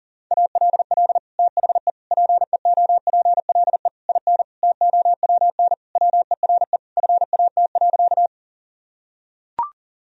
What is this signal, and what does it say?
Morse code, heard as a keyed tone, about 40 wpm, 700 Hz. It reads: ALL THE PEOPLE IN TOWN WERE FAT.
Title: Sentences from Top 1000 Words - Review Part 4 - 40wpm